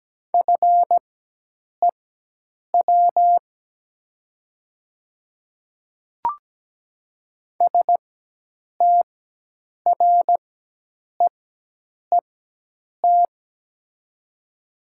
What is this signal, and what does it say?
Morse code, heard as a keyed tone, about 17 wpm, spaced as for 8 wpm, 700 Hz. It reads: FEW STREET
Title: 4x Character Spacing - Top 500 Words - 17wpm